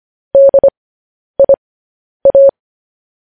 Morse code, heard as a keyed tone, about 25 wpm, spaced as for 10 wpm, 550 Hz. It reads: DIA
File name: podcast / News headlines-1x-25x10